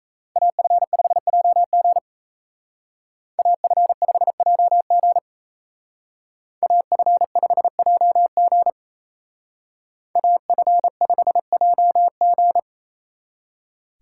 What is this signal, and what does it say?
Morse code, heard as a keyed tone, about 32 wpm, 700 Hz. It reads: AF5JG AF5JG AF5JG AF5JG
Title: Speed Racing - USA 2x2 Call Signs 28wpm